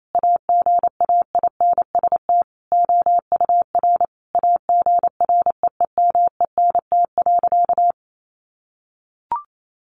Morse code, heard as a keyed tone, about 28 wpm, 700 Hz. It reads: AGAINST OUR AGREEMENT.
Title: Sentences from Top 1500 Words 28wpm